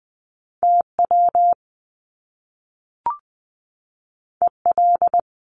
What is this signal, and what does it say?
Morse code, heard as a keyed tone, about 20 wpm, 700 Hz. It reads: TW EL
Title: Top Two Letter Combinations in Words 20wpm